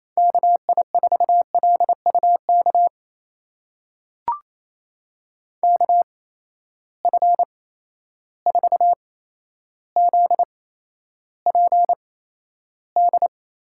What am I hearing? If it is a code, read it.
Morse code, heard as a keyed tone, about 28 wpm, spaced as for 12 wpm, 700 Hz. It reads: KI4LUK KF4ZPD
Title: USA Call Signs - 6 Characters - 8x Character Spacing - 28wpm